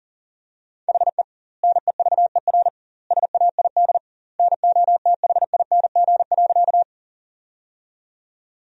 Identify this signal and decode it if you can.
Morse code, heard as a keyed tone, about 40 wpm, 700 Hz. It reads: HE NEVER SAID NOTHING.
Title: 2x Word-Spacing - Sentences from Top 500 Words - 40wpm